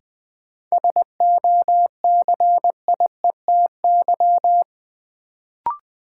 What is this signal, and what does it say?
Morse code, heard as a keyed tone, about 20 wpm, 700 Hz. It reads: SOCIETY
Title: Sentences from Top 700 Words - New Word Drill - 20wpm